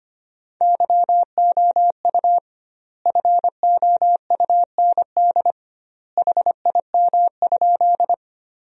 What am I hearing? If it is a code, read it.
Morse code, heard as a keyed tone, about 25 wpm, 700 Hz. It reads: YOU FOUND HIM?
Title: 2x Word-Spacing - Sentences from Top 300 Words - 25wpm